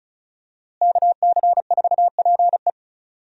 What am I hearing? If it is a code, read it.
Morse code, heard as a keyed tone, about 34 wpm, 700 Hz. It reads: KC4PE
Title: USA Call Signs - 5 Characters - Repeated 2x - 35wpm